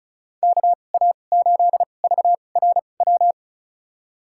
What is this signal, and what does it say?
Morse code, heard as a keyed tone, about 36 wpm, spaced as for 25 wpm, 690 Hz. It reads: KA8VRW
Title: USA Call Signs - 6 Characters - 2x Character Spacing - 35wpm